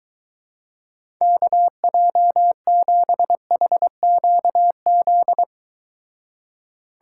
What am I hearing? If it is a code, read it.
Morse code, heard as a keyed tone, about 23 wpm, 700 Hz. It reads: KJ7HQZ